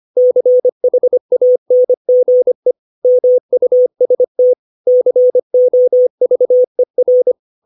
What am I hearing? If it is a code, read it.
Morse code, heard as a keyed tone, about 25 wpm, 500 Hz. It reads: CHANGE MUST COVER
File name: 632 words / 632 Words 25WPM